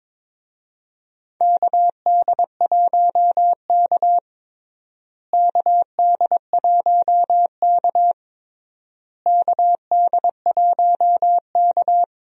Morse code, heard as a keyed tone, about 22 wpm, 700 Hz. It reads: KD1K KD1K KD1K
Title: USA Call Signs - 4 Characters - Repeated 3x - 22wpm